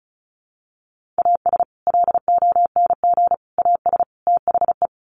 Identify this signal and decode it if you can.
Morse code, heard as a keyed tone, about 34 wpm, 700 Hz. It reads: AS LONG AS THE
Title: Sets of 4 Words 35wpm